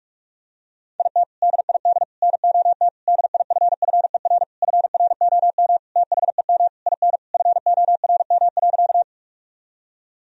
Morse code, heard as a keyed tone, about 41 wpm, 700 Hz. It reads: IT DID NOT DIFFER FROM THEM IN FORM.
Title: Sentences from Top 1000 Words - Review Part 1 - 45wpm